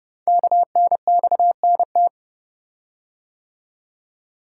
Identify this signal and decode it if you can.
Morse code, heard as a keyed tone, about 30 wpm, 700 Hz. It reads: KNXNT
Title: Code Groups - 5 Letters - 30wpm